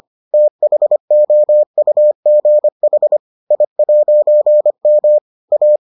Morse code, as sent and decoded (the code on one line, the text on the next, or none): - .... --- ..- --. .... / .. .----. -- / .-
THOUGH I'M A